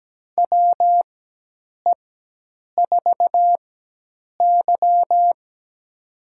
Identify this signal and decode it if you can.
Morse code, heard as a keyed tone, about 17 wpm, spaced as for 8 wpm, 700 Hz. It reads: WE4Y
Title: USA Call Signs - 4x Character Spacing - 17wpm